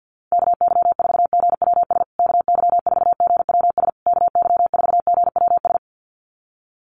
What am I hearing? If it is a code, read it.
Morse code, heard as a keyed tone, about 48 wpm, 700 Hz. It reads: KY4GWS KY4GWS KY4GWS
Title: USA Call Signs Repeated 3x 50wpm